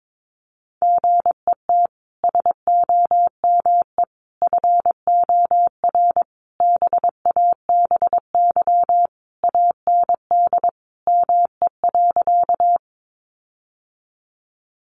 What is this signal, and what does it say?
Morse code, heard as a keyed tone, about 22 wpm, 700 Hz. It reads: GET SOME FOR BABY AND ME.
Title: Sentences from Top 1300 Words 22wpm